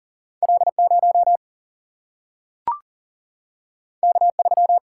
Morse code, heard as a keyed tone, about 40 wpm, 700 Hz. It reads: L0 K3
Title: Single-Letter Single-Number 40wpm